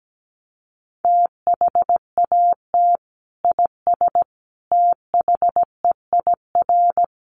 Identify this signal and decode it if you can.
Morse code, heard as a keyed tone, about 17 wpm, 700 Hz. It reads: THAT IS THEIR
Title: Sentences from Top 1300 Words 17wpm